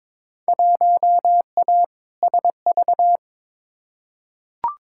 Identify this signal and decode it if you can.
Morse code, heard as a keyed tone, about 22 wpm, 700 Hz. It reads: 1A SV
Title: Field Day Exchanges - 22wpm